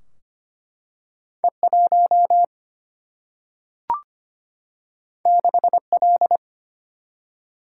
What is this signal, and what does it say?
Morse code, heard as a keyed tone, about 25 wpm, 700 Hz. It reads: E 1 6 L
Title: Single-Letter Single-Number 25wpm